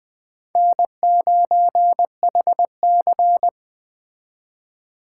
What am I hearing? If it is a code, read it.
Morse code, heard as a keyed tone, about 20 wpm, 700 Hz. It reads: N9HC